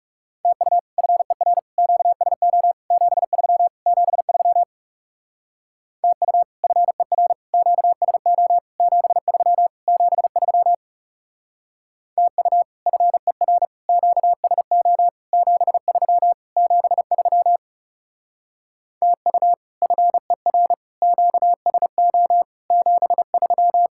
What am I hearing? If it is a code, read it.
Morse code, heard as a keyed tone, about 40 wpm, 700 Hz. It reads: TU FER QSO 73 73 TU FER QSO 73 73 TU FER QSO 73 73 TU FER QSO 73